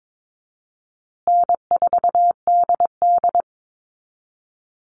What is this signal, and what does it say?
Morse code, heard as a keyed tone, about 22 wpm, 700 Hz. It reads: N4DD
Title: Easy Call Signs Repeated 2x - 22wpm